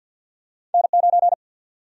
Morse code, heard as a keyed tone, about 52 wpm, 700 Hz. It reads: N9
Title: USA Call Sign Prefixes - 50wpm